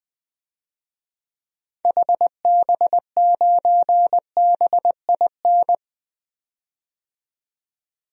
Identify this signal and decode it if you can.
Morse code, heard as a keyed tone, about 20 wpm, 700 Hz. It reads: HB9BIN
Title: DXCC-100 Call Signs - 6 Characters - 20wpm